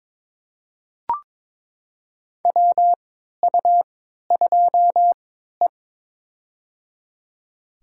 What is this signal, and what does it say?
Morse code, heard as a keyed tone, about 22 wpm, spaced as for 13 wpm, 700 Hz. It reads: WU2E
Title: USA Call Signs - 4 Characters - 3x Character Spacing - 22wpm